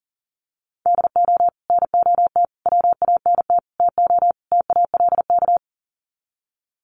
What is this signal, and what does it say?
Morse code, heard as a keyed tone, about 40 wpm, 700 Hz. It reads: DO NOT WANT TO TALK
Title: Sets of 5 Words 40wpm